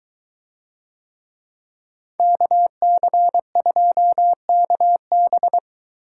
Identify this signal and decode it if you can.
Morse code, heard as a keyed tone, about 22 wpm, 700 Hz. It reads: KC2KB